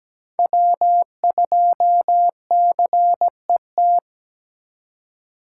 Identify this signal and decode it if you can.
Morse code, heard as a keyed tone, about 17 wpm, 700 Hz. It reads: W2CET
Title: USA Call Signs - 5 Characters - 4x Character Spacing - 17wpm